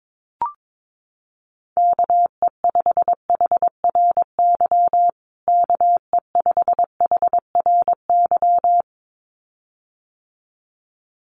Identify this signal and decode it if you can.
Morse code, heard as a keyed tone, about 22 wpm, 700 Hz. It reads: KE5HRY KE5HRY
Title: USA Call Signs Repeated 2x 22wpm